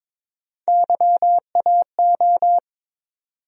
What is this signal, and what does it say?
Morse code, heard as a keyed tone, about 21 wpm, 700 Hz. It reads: YAO
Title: USA Call Sign Postfixes - 3 Letters 22wpm